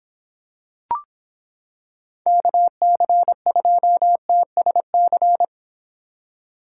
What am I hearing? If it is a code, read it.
Morse code, heard as a keyed tone, about 26 wpm, 700 Hz. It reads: KC2TSC